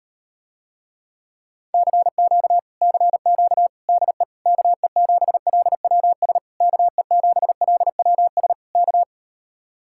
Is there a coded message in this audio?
Yes: Morse code from 1.7 to 9.0 s, heard as a keyed tone, about 38 wpm, 700 Hz. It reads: CQ CQ DE KE7LWS KE7LWS K